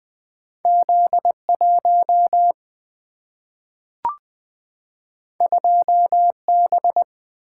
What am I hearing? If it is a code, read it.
Morse code, heard as a keyed tone, about 20 wpm, 700 Hz. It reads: Z1 2B
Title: Single-Letter Single-Number 20wpm